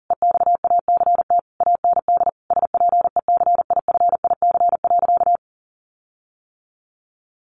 Morse code, heard as a keyed tone, about 40 wpm, 700 Hz. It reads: EXACT AND SPECIFIC.
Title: Sentences from Top 1300 Words 40wpm